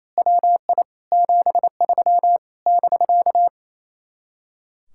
Morse code, heard as a keyed tone, about 26 wpm, 700 Hz. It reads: WI 73 <BK>
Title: Speed-Racing Common QSO Phrases Encore - 28wpm